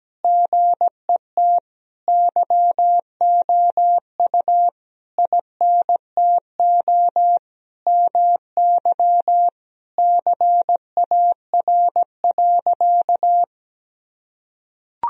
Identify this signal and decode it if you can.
Morse code, heard as a keyed tone, about 17 wpm, 700 Hz. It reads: GET YOU INTO MY CAR.
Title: Sentences from Top 1000 Words - Review Part 1 - 17wpm